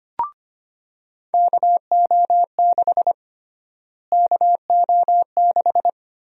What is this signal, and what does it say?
Morse code, heard as a keyed tone, about 25 wpm, 700 Hz. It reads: KO6 KO6
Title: USA Call Sign Prefixes - Repeated 2x - 25wpm